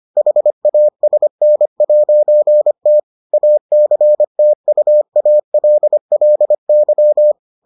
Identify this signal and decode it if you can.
Morse code, heard as a keyed tone, about 25 wpm, 600 Hz. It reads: HASN'T ACTUALLY